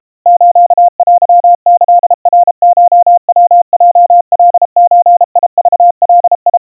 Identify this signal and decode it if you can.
Morse code (Japanese Wabun code), heard as a keyed tone, about 32 wpm, 700 Hz. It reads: ステキナコヤヲカゾクガ
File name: kawa / Morse sample / kamyu 120 cpm 700 Hz Crispy